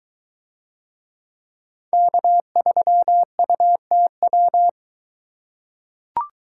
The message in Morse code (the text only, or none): K3UTW